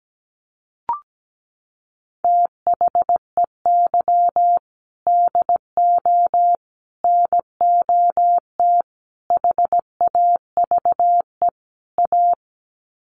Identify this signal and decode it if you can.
Morse code, heard as a keyed tone, about 17 wpm, 700 Hz. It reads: THEY DO NOT HAVE A
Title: Sets of 5 Words 17wpm